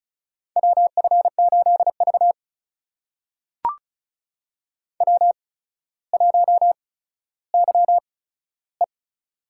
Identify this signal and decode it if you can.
Morse code, heard as a keyed tone, about 36 wpm, spaced as for 15 wpm, 700 Hz. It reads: WF8V W1YE
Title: USA Call Signs - 4 Characters - 8x Character Spacing - 35wpm